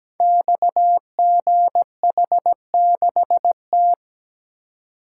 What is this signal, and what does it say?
Morse code, heard as a keyed tone, about 17 wpm, 700 Hz. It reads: XGH6T
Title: Code Groups - 5 Letter:Numbers - 17wpm